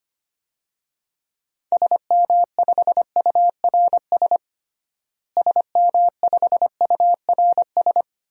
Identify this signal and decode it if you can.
Morse code, heard as a keyed tone, about 25 wpm, 700 Hz. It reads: SM5URS SM5URS
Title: DXCC-100 Call Signs - 6 Characters - Repeated 3x - 25wpm